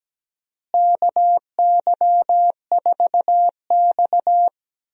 Morse code, heard as a keyed tone, about 17 wpm, 700 Hz. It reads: KY4X
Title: USA Call Signs - 4 Characters - Repeated 3x - 17wpm